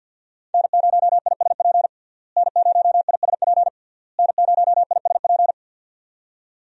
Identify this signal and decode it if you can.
Morse code, heard as a keyed tone, about 52 wpm, 700 Hz. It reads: N0ISP N0ISP N0ISP
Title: USA Call Signs - 5 Characters - Repeated 3x - 50wpm